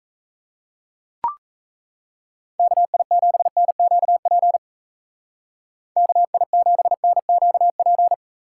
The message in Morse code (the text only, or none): KI7NQP KI7NQP